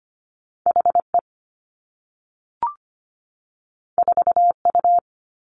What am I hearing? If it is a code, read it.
Morse code, heard as a keyed tone, about 25 wpm, 700 Hz. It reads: HE 4U
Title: Two Challenging Letter-Number Combinations 25wpm